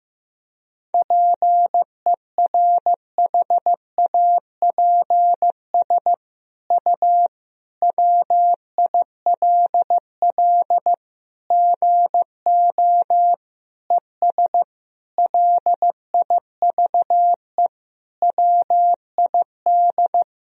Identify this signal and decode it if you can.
Morse code, heard as a keyed tone, about 15 wpm, 700 Hz. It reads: PERHAPS U WILL GO ES LIVE WID
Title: Sentences from Top 500 Words and Abbreviations 15wpm